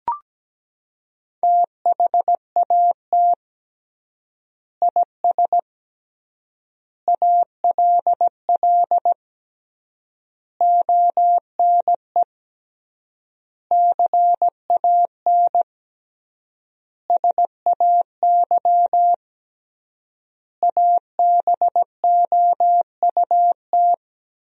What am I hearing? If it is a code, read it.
Morse code, heard as a keyed tone, about 17 wpm, 700 Hz. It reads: THAT IS ALL ONE CAN SAY ABOUT
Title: 3x Word-Spacing - Sentences from Top 200 Words - 17wpm